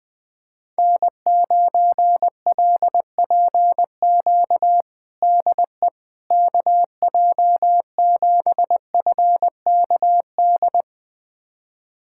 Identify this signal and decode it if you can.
Morse code, heard as a keyed tone, about 21 wpm, 700 Hz. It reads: N9LPQ DE KJ7FKD
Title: USA Callsign DE Callsign 20wpm